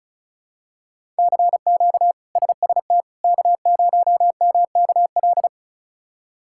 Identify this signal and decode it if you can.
Morse code, heard as a keyed tone, about 36 wpm, 700 Hz. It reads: CQ SST K0MKL
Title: K1USN SST QSO Practice - 35wpm